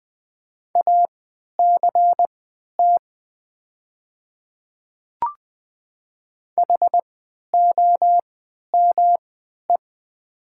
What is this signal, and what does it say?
Morse code, heard as a keyed tone, about 20 wpm, spaced as for 11 wpm, 700 Hz. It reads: ACT HOME
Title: 3x Character Spacing - Top 200 Words - 20wpm